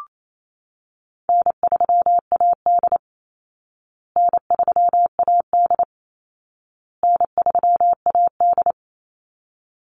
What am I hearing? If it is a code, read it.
Morse code, heard as a keyed tone, about 28 wpm, 700 Hz. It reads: N3AD N3AD N3AD
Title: Easy Call Signs Repeated 3x - 28wpm